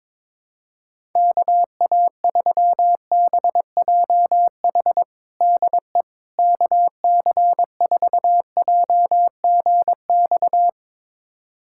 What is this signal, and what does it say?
Morse code, heard as a keyed tone, about 22 wpm, 700 Hz. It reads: KA3BJH DE KC4JGX
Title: USA Callsign DE Callsign - 6 Characters - 22wpm